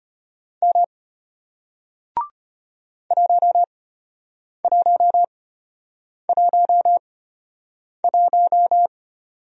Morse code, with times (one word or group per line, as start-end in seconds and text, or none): 0.6-0.8 M
3.1-8.9 1111